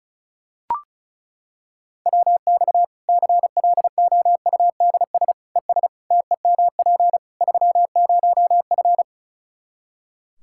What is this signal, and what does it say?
Morse code, heard as a keyed tone, about 36 wpm, 700 Hz. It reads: WX CLOUDS ES TEMP 30F
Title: K1USN SST QSO Practice 35wpm